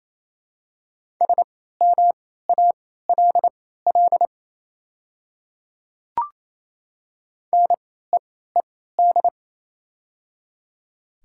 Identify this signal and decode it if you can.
Morse code, heard as a keyed tone, about 28 wpm, spaced as for 16 wpm, 700 Hz. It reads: SMALL NEED
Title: 3x Character Spacing - Top 200 Words - 28wpm